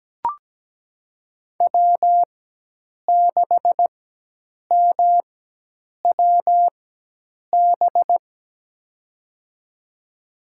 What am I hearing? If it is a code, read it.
Morse code, heard as a keyed tone, about 17 wpm, spaced as for 8 wpm, 700 Hz. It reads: W6MWB
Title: USA Call Signs - 5 Characters - 4x Character Spacing - 17wpm